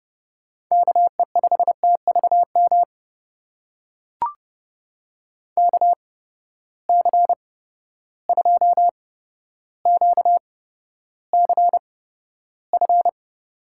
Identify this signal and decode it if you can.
Morse code, heard as a keyed tone, about 30 wpm, spaced as for 13 wpm, 700 Hz. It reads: KE5TVM KC2QCF
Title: USA Call Signs - 6 Characters - 8x Character Spacing - 30wpm